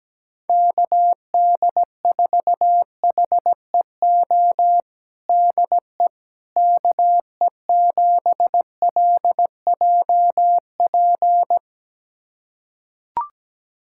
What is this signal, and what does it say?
Morse code, heard as a keyed tone, about 17 wpm, 700 Hz. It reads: KD4HEO DE KE7LJP
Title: USA Callsign DE Callsign 17wpm